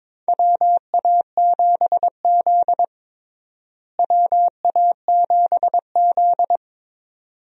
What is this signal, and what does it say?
Morse code, heard as a keyed tone, about 22 wpm, 700 Hz. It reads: WA7Z WA7Z
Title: USA Call Signs - 4 Characters - Repeated 2x - 22wpm